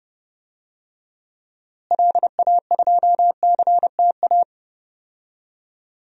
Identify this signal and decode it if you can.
Morse code, heard as a keyed tone, about 31 wpm, 700 Hz. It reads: LA2CTA
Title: DXCC-100 Call Signs - 6 Characters - Repeated 2x - 30wpm